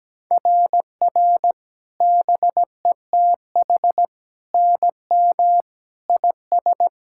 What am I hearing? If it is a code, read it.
Morse code, heard as a keyed tone, about 17 wpm, 700 Hz. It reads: RR BETH NM IS
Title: SOTA QSO Practice 17wpm